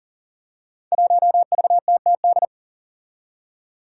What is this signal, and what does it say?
Morse code, heard as a keyed tone, about 40 wpm, 700 Hz. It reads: 1VTTD
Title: Five Challenging Letter-Number Combinations 40wpm